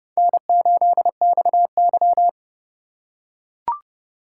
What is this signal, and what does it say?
Morse code, heard as a keyed tone, about 30 wpm, 700 Hz. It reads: N8XY
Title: USA Call Signs - 4 Characters - 8x Character Spacing - 30wpm